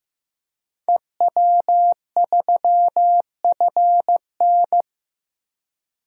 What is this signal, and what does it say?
Morse code, heard as a keyed tone, about 15 wpm, 700 Hz. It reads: EW3FN
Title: DXCC-100 Call Signs - 5 Characters - Repeated 2x - 15wpm